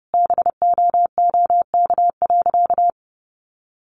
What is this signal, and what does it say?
Morse code, heard as a keyed tone, about 30 wpm, 700 Hz. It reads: BOOK.
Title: Sentences from Top 300 Words 30wpm